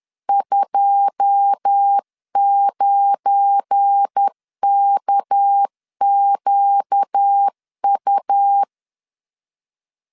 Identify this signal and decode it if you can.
Morse code, heard as a keyed tone, about 10 wpm, 800 Hz. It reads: 29KQU
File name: CW / DET / 1 9 1500C S A 10wpm 4511